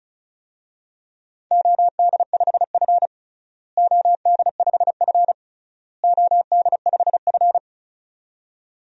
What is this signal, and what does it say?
Morse code, heard as a keyed tone, about 36 wpm, 700 Hz. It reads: OD5F OD5F OD5F
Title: DXCC-100 Call Signs - 4 Characters - Repeated 3x - 35wpm